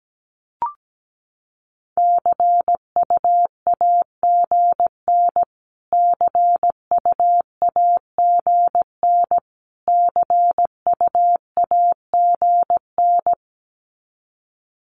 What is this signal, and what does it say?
Morse code, heard as a keyed tone, about 17 wpm, 700 Hz. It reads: CUAGN CUAGN CUAGN
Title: Abbreviations Repeated 3x - 17wpm